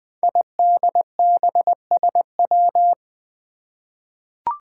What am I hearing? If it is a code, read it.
Morse code, heard as a keyed tone, about 20 wpm, 700 Hz. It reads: IDBSW
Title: Five Challenging Letter-Number Combinations 20wpm